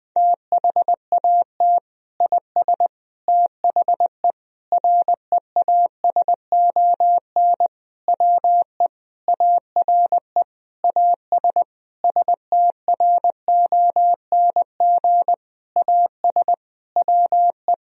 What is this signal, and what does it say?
Morse code, heard as a keyed tone, about 20 wpm, 700 Hz. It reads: THAT IS THE REASON WE ARE AS STRONG AS WE
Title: Sentences from Top 1000 Words - Review Part 4 - 20wpm